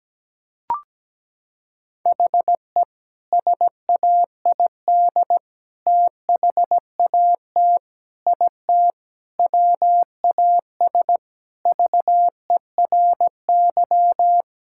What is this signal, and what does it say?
Morse code, heard as a keyed tone, about 17 wpm, 700 Hz. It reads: HE SAID THAT IT WAS VERY
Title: Sentences from Top 1000 Words - Review Part 6 - 17wpm